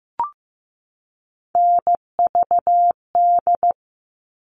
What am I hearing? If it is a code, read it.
Morse code, heard as a keyed tone, about 15 wpm, 700 Hz. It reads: NVD
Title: All Cut Numbers - 3-Digit 15wpm